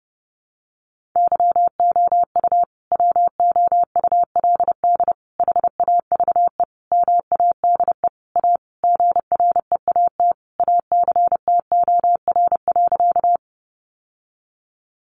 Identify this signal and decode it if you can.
Morse code, heard as a keyed tone, about 30 wpm, 700 Hz. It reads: YOU WOULD HAVE MADE A GREAT ACTOR.
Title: Sentences from Top 1500 Words 30wpm